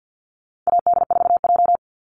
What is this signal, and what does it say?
Morse code, heard as a keyed tone, about 50 wpm, 700 Hz. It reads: AD4J
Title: Call Signs - 50wpm